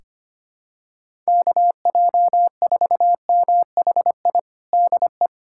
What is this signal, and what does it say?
Morse code, heard as a keyed tone, about 25 wpm, 700 Hz. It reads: KJ4MHI DE
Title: USA Callsign DE Callsign - 6 Characters - 25wpm